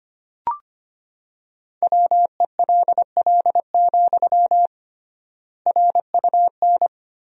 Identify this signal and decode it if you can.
Morse code, heard as a keyed tone, about 25 wpm, 700 Hz. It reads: WELL, RUN
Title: 3x Word-Spacing - Sentences from Top 300 Words - 25wpm